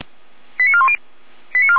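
Five-tone selective call (ZVEI), heard as quick runs of five tones, tones of about 70 ms. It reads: 89410
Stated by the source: ZVEI five-tone selective call received by SDR.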